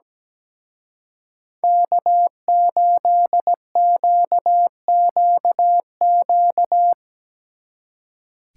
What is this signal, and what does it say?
Morse code, heard as a keyed tone, about 18 wpm, 700 Hz. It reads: K8QQQ
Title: Speed Racing - USA 1x3 Call Signs 17wpm